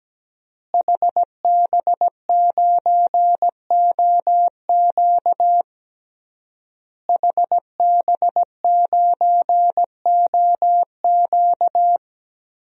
Morse code, heard as a keyed tone, about 17 wpm, 700 Hz. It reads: HB9OQ HB9OQ
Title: DXCC-100 Call Signs - 5 Characters - Repeated 3x - 17wpm